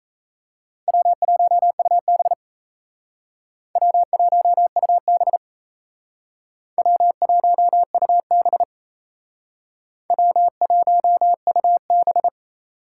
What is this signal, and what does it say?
Morse code, heard as a keyed tone, about 38 wpm, 700 Hz. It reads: W1UB W1UB W1UB W1UB